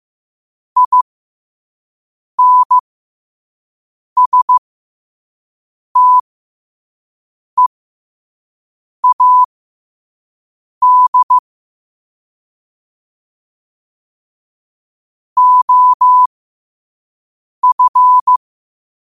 Morse code, heard as a keyed tone, about 15 wpm, spaced as for 5 wpm, 1000 Hz. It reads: INSTEAD OF
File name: qotdfiles / 5wpm